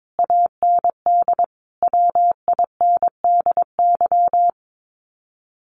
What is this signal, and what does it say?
Morse code, heard as a keyed tone, about 22 wpm, 700 Hz. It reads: AND WINDY
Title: Common QSO Phrases Encore 22wpm